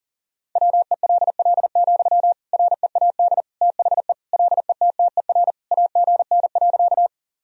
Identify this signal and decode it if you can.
Morse code, heard as a keyed tone, about 40 wpm, 700 Hz. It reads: WELL, READ THE LETTER AGN.
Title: Sentences from Top 300 Words and Abbreviations 40wpm